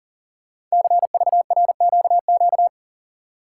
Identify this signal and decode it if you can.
Morse code, heard as a keyed tone, about 41 wpm, 700 Hz. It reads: CVRQQ
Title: Code Groups - 5 Letters - 40wpm